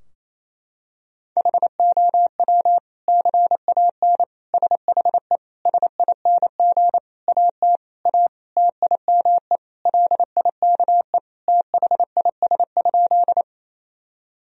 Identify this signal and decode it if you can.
Morse code, heard as a keyed tone, about 28 wpm, 700 Hz. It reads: HOW CAN SHE SING AT A TIME LIKE THIS?
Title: Sentences from Top 1000 Words - Review Part 2 - 28wpm